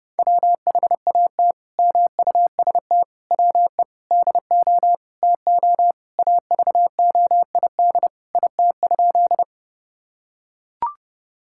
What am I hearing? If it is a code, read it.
Morse code, heard as a keyed tone, about 30 wpm, 700 Hz. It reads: WHAT MUST WE DO TO AVOID IT?
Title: Sentences from Top 1000 Words - Review Part 6 - 30wpm